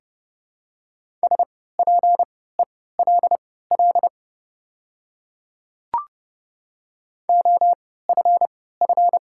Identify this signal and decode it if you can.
Morse code, heard as a keyed tone, about 30 wpm, spaced as for 17 wpm, 700 Hz. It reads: SPELL OFF
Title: 3x Character Spacing - Top 500 Words - 30wpm